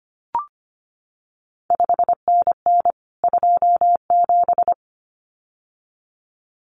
Morse code, heard as a keyed tone, about 25 wpm, 700 Hz. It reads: 5NN 27
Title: CQ WW DX Contest Exchange Encore - 25wpm